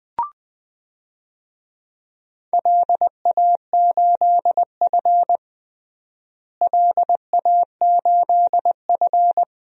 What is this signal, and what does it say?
Morse code, heard as a keyed tone, about 21 wpm, 700 Hz. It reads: LA8F LA8F
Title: DXCC-100 Call Signs - 4 Characters - Repeated 3x - 20wpm